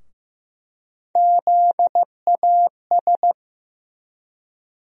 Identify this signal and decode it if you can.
Morse code, heard as a keyed tone, about 15 wpm, 700 Hz. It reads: ZAS
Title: USA Call Sign Postfixes - 3 Letters 15wpm